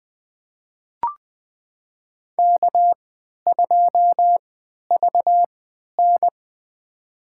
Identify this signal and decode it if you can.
Morse code, heard as a keyed tone, about 20 wpm, spaced as for 11 wpm, 700 Hz. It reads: K2VN
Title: USA Call Signs - 4 Characters - 3x Character Spacing - 20wpm